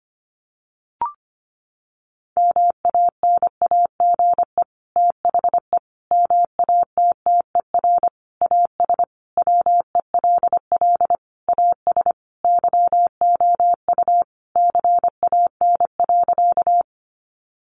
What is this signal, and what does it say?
Morse code, heard as a keyed tone, about 25 wpm, 700 Hz. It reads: MANAGE THE MATTER AS WELL AS YOU CAN.
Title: Sentences from Top 1400 Words 25wpm